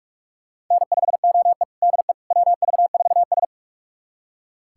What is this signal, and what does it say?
Morse code, heard as a keyed tone, about 45 wpm, 700 Hz. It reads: N5OE DE WV4S
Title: USA Callsign DE Callsign - 4 Characters - 45wpm